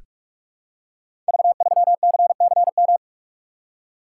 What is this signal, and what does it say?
Morse code, heard as a keyed tone, about 44 wpm, 700 Hz. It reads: V3CCM
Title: Code Groups - 5 Letter:Numbers - 45wpm